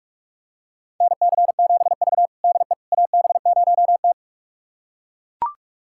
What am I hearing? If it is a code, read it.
Morse code, heard as a keyed tone, about 41 wpm, 700 Hz. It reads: NC7V DE AB0T
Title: USA Callsign DE Callsign - 4 Characters - 45wpm